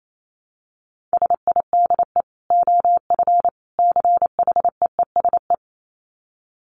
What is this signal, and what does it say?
Morse code, heard as a keyed tone, about 28 wpm, 700 Hz. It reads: SIDE OF CHEESE
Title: Dit Challenge 28wpm